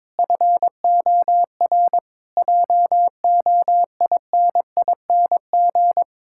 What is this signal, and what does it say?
Morse code, heard as a keyed tone, about 21 wpm, 700 Hz. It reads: FOR JOINING
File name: Build A Phrase - 5-Word Phrases - 22wpm